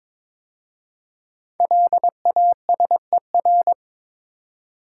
Morse code, heard as a keyed tone, about 22 wpm, 700 Hz. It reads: LASER
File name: 5 Letter Words - First 3 Letters Given - 22wpm